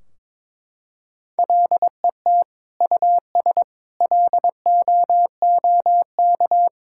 Morse code, heard as a keyed tone, about 22 wpm, 700 Hz. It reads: LET US LOOK
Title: Sentences from Top 1600 Words 22wpm